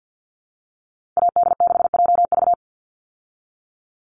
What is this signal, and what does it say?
Morse code, heard as a keyed tone, about 50 wpm, 700 Hz. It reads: AD6JV
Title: Call Signs Repeated 3x - 50wpm